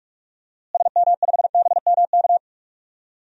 Morse code, heard as a keyed tone, about 42 wpm, 700 Hz. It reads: SM5BMK
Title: DXCC-100 Call Signs - 6 Characters - Repeated 2x - 45wpm